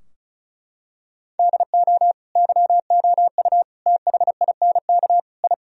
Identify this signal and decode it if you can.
Morse code, heard as a keyed tone, about 34 wpm, 700 Hz. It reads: DO YOU THINK I
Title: Sentences from Top 1000 Words - Review Part 2 - 35wpm